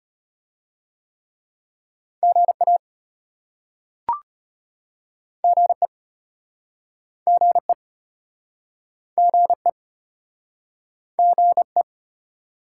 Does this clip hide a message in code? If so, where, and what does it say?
Yes: Morse code from 2.2 to 11.8 s, heard as a keyed tone, about 35 wpm, 700 Hz. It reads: GA GE GE GE GE